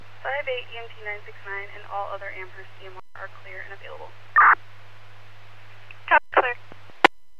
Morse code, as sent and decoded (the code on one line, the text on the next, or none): none